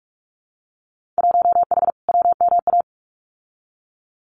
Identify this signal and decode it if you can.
Morse code, heard as a keyed tone, about 46 wpm, 700 Hz. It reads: 1H WMA
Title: Winter Field Day Exchanges 45wpm